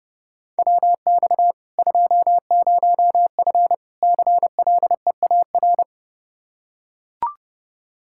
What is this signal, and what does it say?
Morse code, heard as a keyed tone, about 31 wpm, 700 Hz. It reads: WX 20F CLEAR
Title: K1USN SST QSO Practice 30wpm